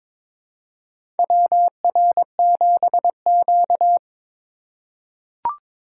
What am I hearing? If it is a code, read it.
Morse code, heard as a keyed tone, about 22 wpm, 700 Hz. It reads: WR7Q
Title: USA Call Signs - 4x Character Spacing - 22wpm